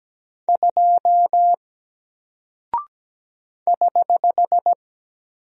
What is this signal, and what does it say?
Morse code, heard as a keyed tone, about 17 wpm, 700 Hz. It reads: 2 <HH>
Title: Single Character (Letter:Number:Punctuation:Prosign) 17wpm